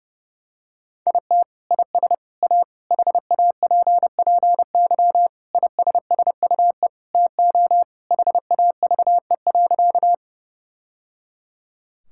Morse code, heard as a keyed tone, about 30 wpm, 700 Hz. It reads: IT IS A HAPPY ISSUE TO HAVE.
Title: Sentences from Top 1000 Words - Review Part 6 - 30wpm